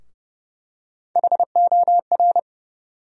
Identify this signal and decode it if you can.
Morse code, heard as a keyed tone, about 30 wpm, 700 Hz. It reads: HOR
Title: USA Call Sign Postfixes - 3 Letters 30wpm